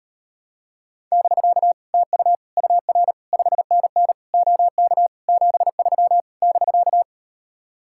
Morse code, heard as a keyed tone, about 38 wpm, 700 Hz. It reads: <BK> TU UR 5NN OK 73 <BK>